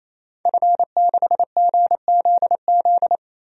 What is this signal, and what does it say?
Morse code, heard as a keyed tone, about 28 wpm, 700 Hz. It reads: F6GZZ
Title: DXCC-100 Call Signs - 5 Characters - Repeated 3x - 28wpm